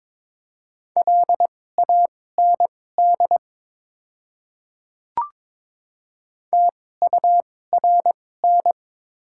Morse code, heard as a keyed tone, about 22 wpm, spaced as for 16 wpm, 700 Hz. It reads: LAND TURN
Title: 2x Character Spacing - Top 200 Words - 22wpm